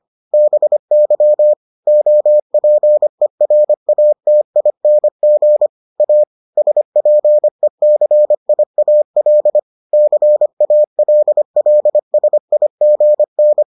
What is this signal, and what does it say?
Morse code, heard as a keyed tone, about 25 wpm, 600 Hz. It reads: BY OPERATING A SPECIAL CALLSIGN